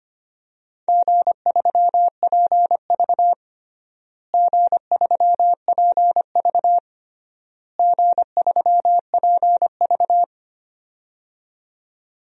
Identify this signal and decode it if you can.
Morse code, heard as a keyed tone, about 25 wpm, 700 Hz. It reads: G3PV G3PV G3PV
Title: DXCC-100 Call Signs - 4 Characters - Repeated 3x - 25wpm